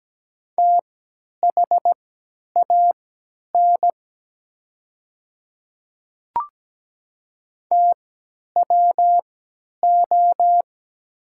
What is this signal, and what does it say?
Morse code, heard as a keyed tone, about 17 wpm, spaced as for 10 wpm, 700 Hz. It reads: THAN TWO
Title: 3x Character Spacing - Top 500 Words - 17wpm